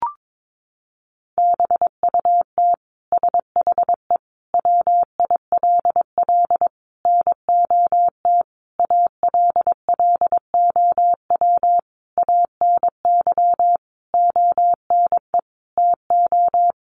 Morse code, heard as a keyed tone, about 22 wpm, 700 Hz. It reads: BUT SHE WILL NOT ALLOW ANY ONE TO
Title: Sentences from Top 1200 Words 22wpm